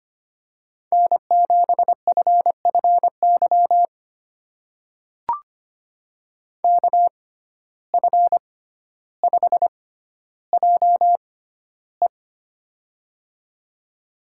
Morse code, heard as a keyed tone, about 25 wpm, spaced as for 13 wpm, 700 Hz. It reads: N7FFY KF5JE
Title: USA Call Signs - 5 Characters - 6x Character Spacing - 25wpm